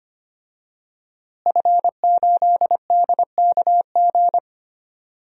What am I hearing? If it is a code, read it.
Morse code, heard as a keyed tone, about 25 wpm, 700 Hz. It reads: F8DKG
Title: DXCC-100 Call Signs - 5 Characters - Repeated 2x - 25wpm